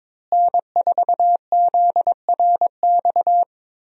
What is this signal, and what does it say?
Morse code, heard as a keyed tone, about 22 wpm, 700 Hz. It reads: N4ZRX
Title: USA Call Signs - 5 Characters - Repeated 3x - 22wpm